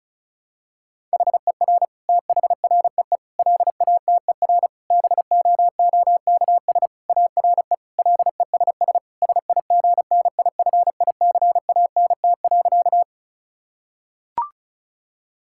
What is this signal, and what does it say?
Morse code, heard as a keyed tone, about 36 wpm, 700 Hz. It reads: HER THREE LATER BOOKS ARE LESS SIGNIFICANT.
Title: Sentences from Top 1000 Words - Review Part 6 - 35wpm